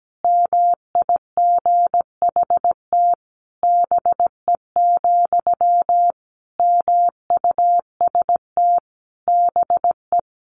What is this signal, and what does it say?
Morse code, heard as a keyed tone, about 17 wpm, 700 Hz. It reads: MIGHT BE, MUST BE
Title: Sentences from Top 1100 Words 17wpm